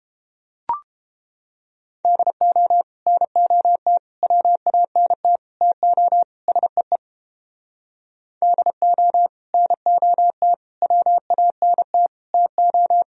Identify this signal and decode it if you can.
Morse code, heard as a keyed tone, about 30 wpm, 700 Hz. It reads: DO NOT WANT TO SEE DO NOT WANT TO